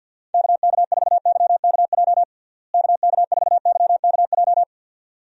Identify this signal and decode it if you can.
Morse code, heard as a keyed tone, about 47 wpm, 700 Hz. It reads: KK4YKJ KK4YKJ
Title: USA Call Signs - 6 Characters - Repeated 2x - 50wpm